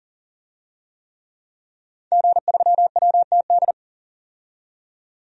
Morse code, heard as a keyed tone, about 40 wpm, 700 Hz. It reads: G3WTD
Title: DXCC-100 Call Signs - 5 Characters - 40wpm